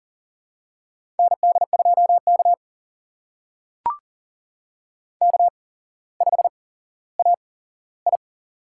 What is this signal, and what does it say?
Morse code, heard as a keyed tone, about 40 wpm, spaced as for 17 wpm, 690 Hz. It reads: ND2K K5AI
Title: USA Call Signs - 4 Characters - 8x Character Spacing - 40wpm